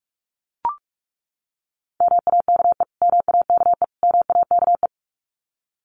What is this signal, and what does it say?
Morse code, heard as a keyed tone, about 44 wpm, 700 Hz. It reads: MAKE MAKE MAKE
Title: Top 500 Words Repeated 45wpm